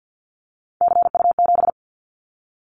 Morse code, heard as a keyed tone, about 50 wpm, 700 Hz. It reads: CU7
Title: Callsign Prefixes for Top 100 DXCC - 50wpm